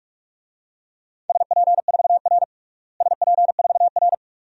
Morse code, heard as a keyed tone, about 46 wpm, 700 Hz. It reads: SP4R SP4R
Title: DXCC-100 Call Signs - 4 Characters - Repeated 2x - 45wpm